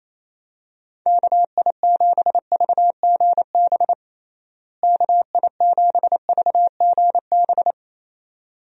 Morse code, heard as a keyed tone, about 28 wpm, 700 Hz. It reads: KI7VGB KI7VGB
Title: USA Call Signs - 6 Characters - Repeated 2x - 28wpm